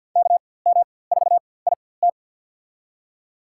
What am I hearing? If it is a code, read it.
Morse code, heard as a keyed tone, about 46 wpm, spaced as for 23 wpm, 700 Hz. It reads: KM4IT
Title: USA Call Signs - 5 Characters - 4x Character Spacing - 50wpm